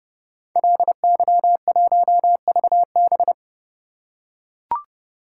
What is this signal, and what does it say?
Morse code, heard as a keyed tone, about 30 wpm, 700 Hz. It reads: LY1VB
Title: Code Groups - 5 Letter:Numbers - 30wpm